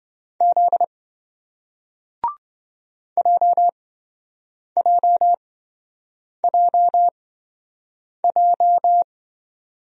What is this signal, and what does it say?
Morse code, heard as a keyed tone, about 27 wpm, spaced as for 7 wpm, 700 Hz. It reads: Z JJJJ